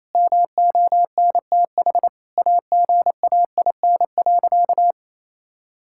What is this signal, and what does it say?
Morse code, heard as a keyed tone, about 28 wpm, 700 Hz. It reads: MONTH AGAIN.
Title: Sentences from Top 1000 Words - Review Part 5 - 28wpm